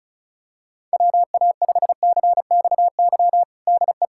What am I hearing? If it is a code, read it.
Morse code, heard as a keyed tone, about 34 wpm, 700 Hz. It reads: WA5CXY DE
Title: USA Callsign DE Callsign 35wpm